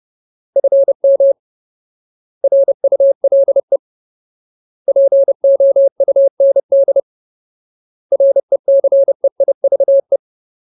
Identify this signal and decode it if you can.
Morse code, heard as a keyed tone, about 30 wpm, 550 Hz. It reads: FM RULE POUND RECEIVE
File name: Ultimate Word List / 30WPM Ultimate